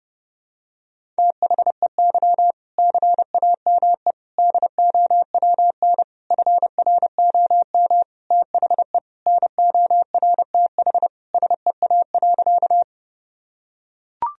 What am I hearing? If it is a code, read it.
Morse code, heard as a keyed tone, about 30 wpm, 700 Hz. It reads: THEY CAME DOWN FROM THE NORTH SEA.
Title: Sentences from Top 1000 Words - Review Part 1 - 30wpm